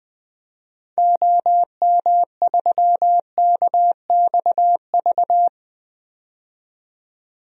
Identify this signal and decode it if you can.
Morse code, heard as a keyed tone, about 21 wpm, 700 Hz. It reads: OM3KXV
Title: DXCC-100 Call Signs - 6 Characters - 20wpm